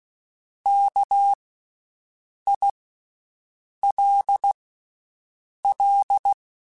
Morse code, heard as a keyed tone, about 16 wpm, 800 Hz. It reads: K I L L
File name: Special INGs-mp3s / Special INGs-mp3s / Slow ing @15